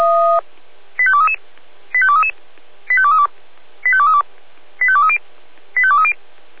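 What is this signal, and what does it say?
Five-tone selective call (ZVEI), heard as quick runs of five tones, tones of about 70 ms. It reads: 85233 85233 85232 85232 85230 85230
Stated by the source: ZVEI five-tone selective call 85885, received by SDR, from the Feuerwehr Orlamünde audio archive.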